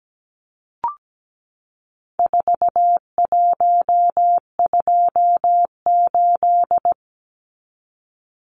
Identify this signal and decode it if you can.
Morse code, heard as a keyed tone, about 17 wpm, 700 Hz. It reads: 4128
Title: Numbers - 4-Digit 17wpm